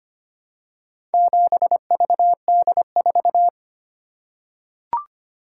Morse code, heard as a keyed tone, about 25 wpm, 700 Hz. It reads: 7VD4
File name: practice / 25wpm / Four Letter-Number Combinations - 25wpm